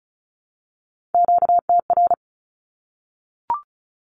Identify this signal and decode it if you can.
Morse code, heard as a keyed tone, about 34 wpm, 700 Hz. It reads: QTR
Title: Q-Codes Repeated 3x 35wpm